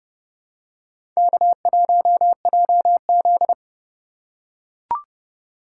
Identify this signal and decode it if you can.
Morse code, heard as a keyed tone, about 29 wpm, 700 Hz. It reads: K1JZ